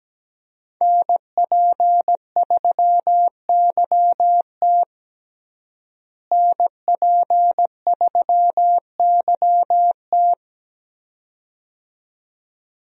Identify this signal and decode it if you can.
Morse code, heard as a keyed tone, about 17 wpm, 700 Hz. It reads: NP3YT NP3YT
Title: USA Call Signs - 5 Characters - Repeated 2x - 17wpm